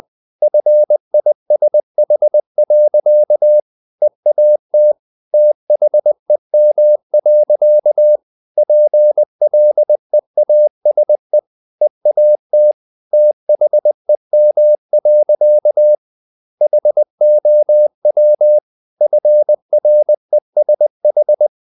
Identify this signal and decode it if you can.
Morse code, heard as a keyed tone, about 20 wpm, 600 Hz. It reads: FISH. EAT THEM. PLEASE EAT THEM. HOW FRESH
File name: the old man and the sea / the old man and the sea part 1 20wpm